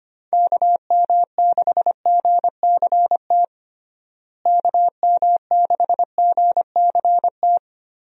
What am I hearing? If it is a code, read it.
Morse code, heard as a keyed tone, about 25 wpm, 700 Hz. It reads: KM6GCT KM6GCT
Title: USA Call Signs - 6 Characters - Repeated 2x - 25wpm